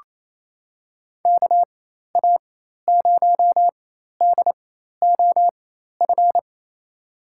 Morse code, heard as a keyed tone, about 27 wpm, spaced as for 13 wpm, 700 Hz. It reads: KA0DOF
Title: USA Call Signs - 6 Characters - 4x Character Spacing - 28wpm